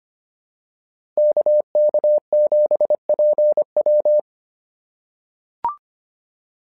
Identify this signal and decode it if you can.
Morse code, heard as a keyed tone, about 25 wpm, 600 Hz. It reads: KK7PW
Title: SOTA QSO Practice 25wpm